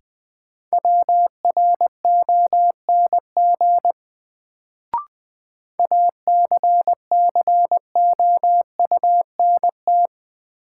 Morse code, heard as a keyed tone, about 20 wpm, 700 Hz. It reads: WRONG ACCOUNT
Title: Sentences from Top 900 Words - New Word Drill - 20wpm